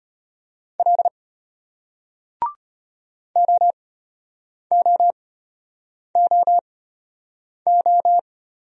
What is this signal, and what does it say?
Morse code, heard as a keyed tone, about 34 wpm, spaced as for 8 wpm, 700 Hz. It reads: L OOOO